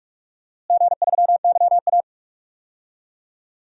Morse code, heard as a keyed tone, about 46 wpm, 700 Hz. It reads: G3YA
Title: DXCC-100 Call Signs - 4 Characters - Repeated 3x - 45wpm